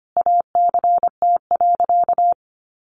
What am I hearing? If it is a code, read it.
Morse code, heard as a keyed tone, about 25 wpm, 700 Hz. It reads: ACT.
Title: Sentences from Top 900 Words 25wpm